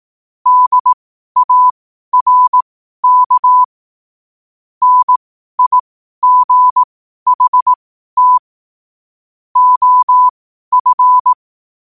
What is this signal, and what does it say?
Morse code, heard as a keyed tone, about 18 wpm, spaced as for 12 wpm, 1000 Hz. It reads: DARK NIGHT OF
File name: qotdfiles / 12wpm